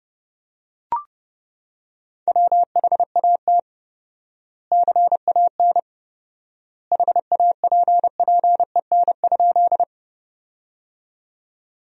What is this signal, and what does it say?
Morse code, heard as a keyed tone, about 31 wpm, 700 Hz. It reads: WHAT CAN HAPPEN?
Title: 4x Word-Spacing - Sentences from Top 400 Words - 30wpm